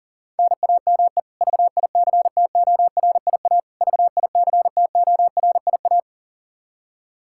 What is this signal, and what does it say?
Morse code, heard as a keyed tone, about 37 wpm, 700 Hz. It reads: NAME VICTORIA VICTORIA
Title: K1USN SST QSO Practice 40wpm